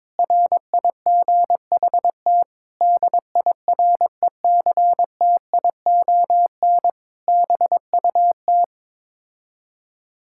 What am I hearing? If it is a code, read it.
Morse code, heard as a keyed tone, about 22 wpm, 700 Hz. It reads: RIGHT DIRECTION BUT
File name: practice / 22wpm / Sets of 8 Words - 22wpm